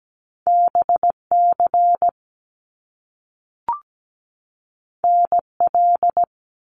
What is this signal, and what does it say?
Morse code, heard as a keyed tone, about 17 wpm, 700 Hz. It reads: BC NL
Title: Canadian Provinces and Territories 17wpm